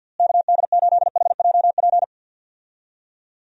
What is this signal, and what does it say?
Morse code, heard as a keyed tone, about 50 wpm, 700 Hz. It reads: KD8HJP
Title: USA Call Signs - 6 Characters - Repeated 2x - 50wpm